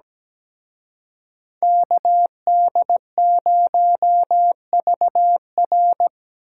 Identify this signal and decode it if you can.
Morse code, heard as a keyed tone, about 20 wpm, 700 Hz. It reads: KD0VR
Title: Speed Racing - USA 2x2 Call Signs 17wpm